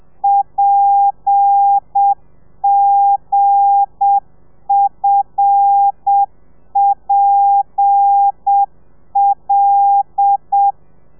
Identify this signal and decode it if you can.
Morse code, heard as a keyed tone, about 7 wpm, 800 Hz. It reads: PGFPL
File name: telegrafia / i2tfj / Lez8 35